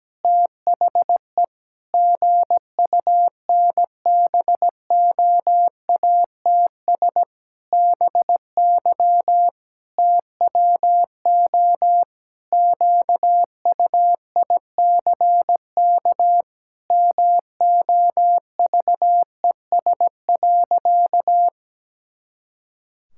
Morse code, heard as a keyed tone, about 17 wpm, 700 Hz. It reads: THE GUNBOATS BY TWO QUICK MOVES.
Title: Pangram Challenge v2 - 17wpm